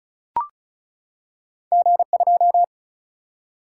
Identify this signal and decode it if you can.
Morse code, heard as a keyed tone, about 36 wpm, 700 Hz. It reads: G2
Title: Single-Letter Single-Number 35wpm